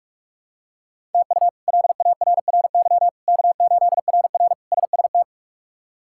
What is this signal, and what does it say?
Morse code, heard as a keyed tone, about 43 wpm, 700 Hz. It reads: TU LARRY K8RR SST
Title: K1USN SST QSO Practice - 45wpm